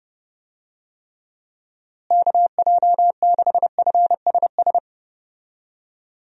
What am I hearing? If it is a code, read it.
Morse code, heard as a keyed tone, about 32 wpm, 700 Hz. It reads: KJ6FSS
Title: USA Call Signs - 6 Characters - Repeated 2x - 30wpm